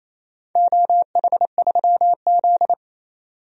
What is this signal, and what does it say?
Morse code, heard as a keyed tone, about 28 wpm, 700 Hz. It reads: OH3Z
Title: DXCC-100 Call Signs - 4 Characters - 28wpm